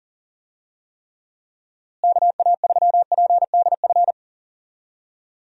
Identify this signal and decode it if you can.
Morse code, heard as a keyed tone, about 40 wpm, 700 Hz. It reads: KA3PDF
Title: USA Call Signs - 6 Characters - Repeated 3x - 40wpm